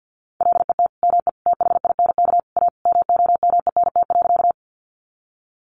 Morse code, heard as a keyed tone, about 50 wpm, 700 Hz. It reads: LET ME THINK A MOMENT.
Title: Sentences from Top 1000 Words 50wpm